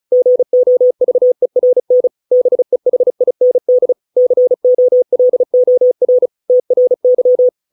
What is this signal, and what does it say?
Morse code, heard as a keyed tone, about 35 wpm, 500 Hz. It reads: GOVERN BEHIND COLOR TRY